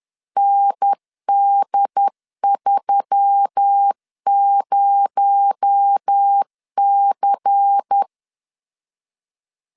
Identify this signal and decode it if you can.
Morse code, heard as a keyed tone, about 10 wpm, 800 Hz. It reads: ND30C
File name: LEZIONI / 8 YJ45 / 8 Y 10wpm 2615